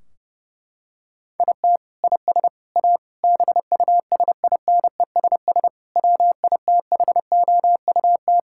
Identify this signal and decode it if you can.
Morse code, heard as a keyed tone, about 30 wpm, 700 Hz. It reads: IT IS A BUSINESS WITHOUT
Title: Sentences from Top 1600 Words 30wpm